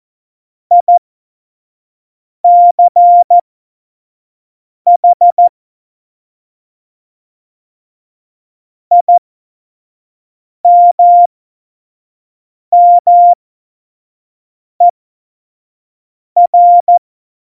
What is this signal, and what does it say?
Morse code, heard as a keyed tone, about 14 wpm, spaced as for 5 wpm, 700 Hz. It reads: ICH IMMER
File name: Morsekurs / Witze / Witz 04. Juni 2019, 25 F70